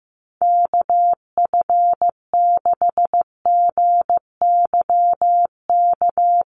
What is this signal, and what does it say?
Morse code, heard as a keyed tone, about 14 wpm, 700 Hz. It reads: KF6GYK
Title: USA Call Signs 15wpm